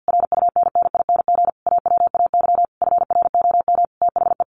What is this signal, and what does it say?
Morse code, heard as a keyed tone, about 50 wpm, 700 Hz. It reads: RUNNING AWAY FROM THE
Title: A Sound Of Thunder 50wpm